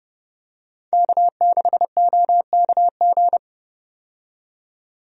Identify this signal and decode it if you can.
Morse code, heard as a keyed tone, about 30 wpm, 700 Hz. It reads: K6OKG
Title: USA Call Signs - 5 Characters - 30wpm